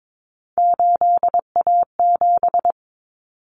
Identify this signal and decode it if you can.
Morse code, heard as a keyed tone, about 22 wpm, 700 Hz. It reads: 8A7
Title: Common Cut Numbers - 3-Digit 22wpm